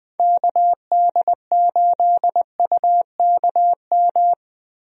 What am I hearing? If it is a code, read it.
Morse code, heard as a keyed tone, about 20 wpm, 700 Hz. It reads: KD8UKM
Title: USA Call Signs - 6 Characters - Repeated 2x - 20wpm